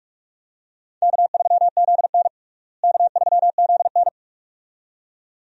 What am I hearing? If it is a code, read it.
Morse code, heard as a keyed tone, about 46 wpm, 700 Hz. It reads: K3ZN K3ZN
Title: USA Call Signs - 4 Characters - Repeated 2x - 45wpm